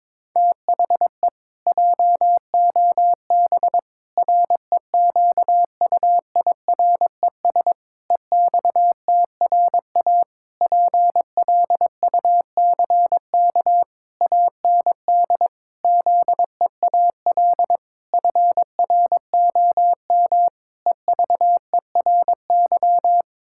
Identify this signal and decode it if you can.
Morse code, heard as a keyed tone, about 21 wpm, 700 Hz. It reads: THE JOB REQUIRES EXTRA PLUCK AND ZEAL FROM EVERY
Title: Pangram Challenge v2 - 22wpm